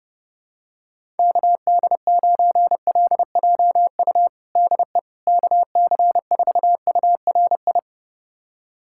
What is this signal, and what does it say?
Morse code, heard as a keyed tone, about 32 wpm, 700 Hz. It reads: KD9LJU DE KC4URI
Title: USA Callsign DE Callsign 30wpm